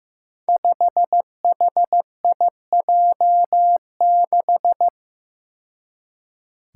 Morse code, heard as a keyed tone, about 15 wpm, 700 Hz. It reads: 5HIJ6
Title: Five Challenging Letter-Number Combinations 15wpm